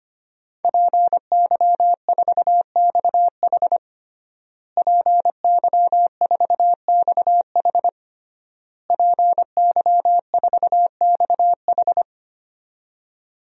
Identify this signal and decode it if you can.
Morse code, heard as a keyed tone, about 25 wpm, 700 Hz. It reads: PY4XH PY4XH PY4XH
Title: DXCC-100 Call Signs - 5 Characters - Repeated 3x - 25wpm